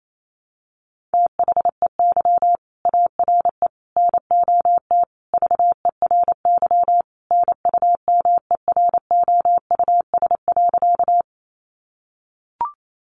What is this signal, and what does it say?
Morse code, heard as a keyed tone, about 28 wpm, 700 Hz. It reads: THEY ARE NOT VERY NUMEROUS.
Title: Sentences from Top 1400 Words 28wpm